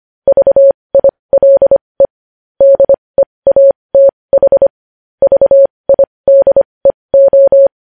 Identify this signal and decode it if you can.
Morse code, heard as a keyed tone, about 25 wpm, spaced as for 20 wpm, 550 Hz. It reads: VILE DEATH VIDEO
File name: podcast / News headlines-1x-25x20